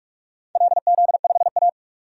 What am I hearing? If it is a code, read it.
Morse code, heard as a keyed tone, about 42 wpm, 700 Hz. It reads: LZ5A
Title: DXCC-100 Call Signs - 4 Characters - 45wpm